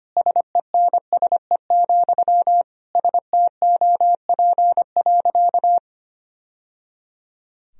Morse code, heard as a keyed tone, about 25 wpm, 700 Hz. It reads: SENSE, STOP.
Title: Sentences from Top 1000 Words - Review Part 4 - 25wpm